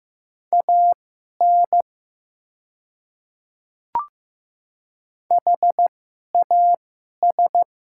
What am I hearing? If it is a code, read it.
Morse code, heard as a keyed tone, about 15 wpm, spaced as for 11 wpm, 700 Hz. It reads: AN HAS
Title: 2x Character Spacing - Top 100 Words - 15wpm